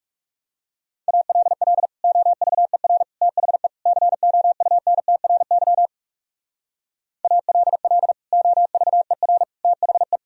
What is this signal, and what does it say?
Morse code, heard as a keyed tone, about 38 wpm, 700 Hz. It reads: ALL OVER THE COUNTRY ALL OVER THE
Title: Speed-Racing - Sets of 4 Words 30wpm